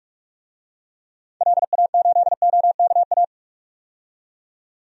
Morse code, heard as a keyed tone, about 43 wpm, 700 Hz. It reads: LA8OKA
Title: DXCC-100 Call Signs - 6 Characters - 45wpm